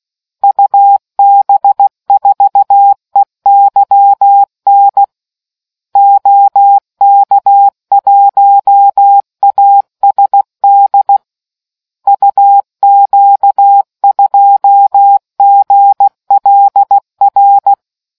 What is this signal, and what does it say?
Morse code, heard as a keyed tone, about 16 wpm, 800 Hz. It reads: UB4EYN OK1ASD UQ2GLR